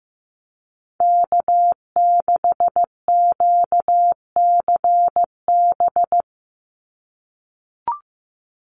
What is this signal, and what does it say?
Morse code, heard as a keyed tone, about 15 wpm, 700 Hz. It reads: K6QCB
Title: NA SOTA Chaser Callsigns 15wpm